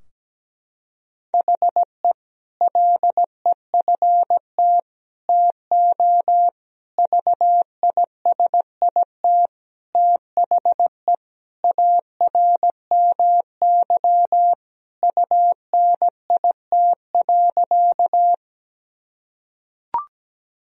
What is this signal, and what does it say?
Morse code, heard as a keyed tone, about 17 wpm, 700 Hz. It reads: HE LEFT TO VISIT THE ARMY UNIT.
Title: Sentences from Top 1000 Words - Review Part 2 - 17wpm